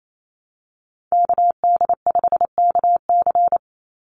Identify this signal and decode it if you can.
Morse code, heard as a keyed tone, about 28 wpm, 700 Hz. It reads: KD5KC
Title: NA SOTA Activator Callsigns 28wpm